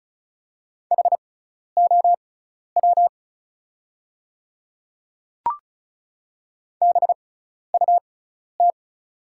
Morse code, heard as a keyed tone, about 36 wpm, spaced as for 12 wpm, 700 Hz. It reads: HOW BUT
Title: 6x Character Spacing - Top 300 Words - 35wpm